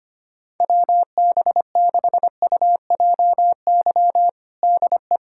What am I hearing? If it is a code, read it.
Morse code, heard as a keyed tone, about 25 wpm, 700 Hz. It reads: WB6UJY DE
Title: USA Callsign DE Callsign 25wpm